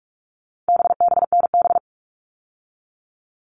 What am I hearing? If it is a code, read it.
Morse code, heard as a keyed tone, about 44 wpm, 700 Hz. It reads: BBNB
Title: All Cut Numbers - 4-Digit 45wpm